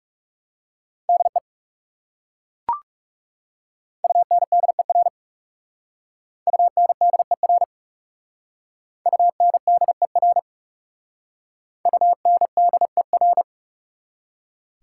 Morse code, heard as a keyed tone, about 37 wpm, 700 Hz. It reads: DE UNDER UNDER UNDER UNDER
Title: Speed-Racing Prefixes and Suffixes 30wpm